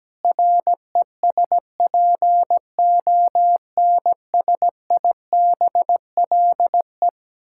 Morse code, heard as a keyed tone, about 17 wpm, 700 Hz. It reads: RESPONSIBLE
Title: Sentences from Top 1600 Words 17wpm